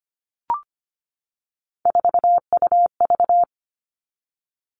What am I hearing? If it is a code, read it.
Morse code, heard as a keyed tone, about 25 wpm, 700 Hz. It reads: 4UV
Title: All Cut Numbers - 3-Digit 25wpm